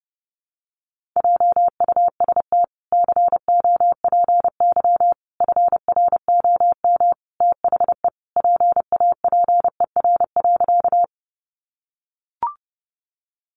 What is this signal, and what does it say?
Morse code, heard as a keyed tone, about 30 wpm, 700 Hz. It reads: JUST COPY FROM THE PAPER.
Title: Sentences from Top 1400 Words 30wpm